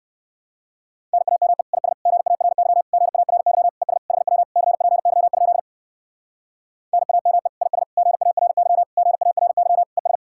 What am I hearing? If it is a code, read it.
Morse code, heard as a keyed tone, about 65 wpm, 700 Hz. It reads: NAME IS GARY GARY ES HW CPY? NAME IS GARY GARY ES